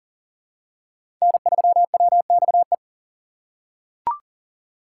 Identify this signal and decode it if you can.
Morse code, heard as a keyed tone, about 40 wpm, 700 Hz. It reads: N3WXE
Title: USA Call Signs - 5 Characters - 2x Character Spacing - 40wpm